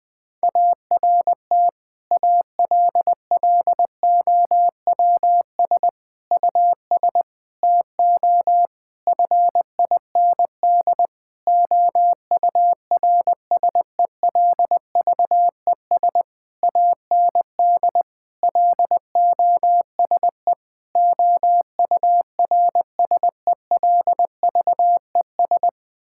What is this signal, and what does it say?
Morse code, heard as a keyed tone, about 20 wpm, 700 Hz. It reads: ART ALLOWS US TO FIND OURSELVES AND LOSE OURSELVES
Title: Difficult Sentences with Hints - 20wpm